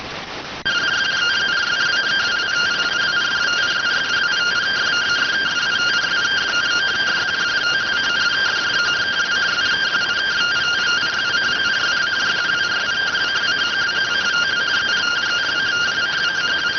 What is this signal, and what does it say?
Two-tone teleprinter signal, about 170 Hz shift.